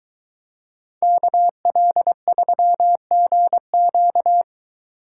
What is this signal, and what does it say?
Morse code, heard as a keyed tone, about 23 wpm, 700 Hz. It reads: KL3GQ